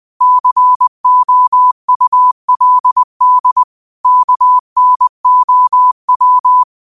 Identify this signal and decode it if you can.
Morse code, heard as a keyed tone, about 20 wpm, 1000 Hz. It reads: COULD KNOW
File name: qotdfiles / 20wpm